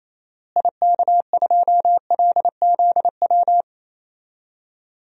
Morse code, heard as a keyed tone, about 27 wpm, 700 Hz. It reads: IK2LZW
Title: DXCC-100 Call Signs - 6 Characters - Repeated 2x - 28wpm